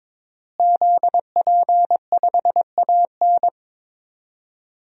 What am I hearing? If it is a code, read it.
Morse code, heard as a keyed tone, about 22 wpm, 700 Hz. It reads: ZP5AN
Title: DXCC-100 Call Signs - 5 Characters - Repeated 3x - 22wpm